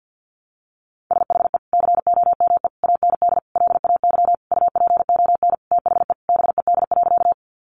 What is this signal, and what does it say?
Morse code, heard as a keyed tone, about 50 wpm, 700 Hz. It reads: SHE COME AND LAY UPON THE BED.
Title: Sentences from Top 1100 Words 50wpm